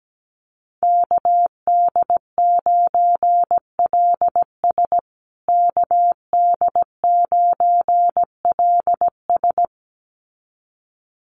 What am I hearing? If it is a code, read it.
Morse code, heard as a keyed tone, about 17 wpm, 700 Hz. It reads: KD9LS KD9LS
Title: USA Call Signs Repeated 2x 17wpm